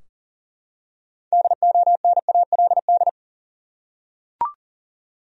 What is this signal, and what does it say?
Morse code, heard as a keyed tone, about 40 wpm, 700 Hz. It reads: DONALD
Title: USA Common Names - 40wpm